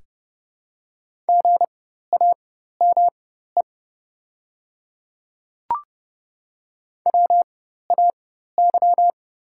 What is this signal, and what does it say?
Morse code, heard as a keyed tone, about 31 wpm, spaced as for 14 wpm, 700 Hz. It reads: GAME WAY
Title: 4x Character Spacing - Top 500 Words - 30wpm